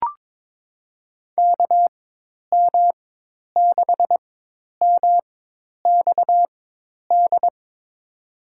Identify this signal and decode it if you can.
Morse code, heard as a keyed tone, about 23 wpm, spaced as for 10 wpm, 700 Hz. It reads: KM6MXD
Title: USA Call Signs - 4x Character Spacing - 22wpm